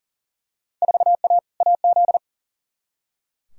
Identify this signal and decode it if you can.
Morse code, heard as a keyed tone, about 40 wpm, 700 Hz. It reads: 4A AZ
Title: Field Day Exchanges - 40wpm